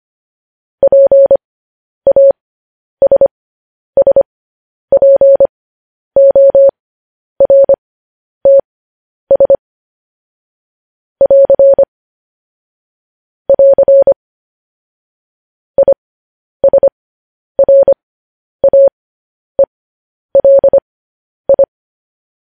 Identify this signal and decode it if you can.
Morse code, heard as a keyed tone, about 25 wpm, spaced as for 10 wpm, 550 Hz. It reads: PASSPORTS <AR> <AR> ISRAELI